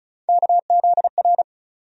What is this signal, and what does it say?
Morse code, heard as a keyed tone, about 34 wpm, 700 Hz. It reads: KZR
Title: USA Call Sign Postfixes - 3 Letters 35wpm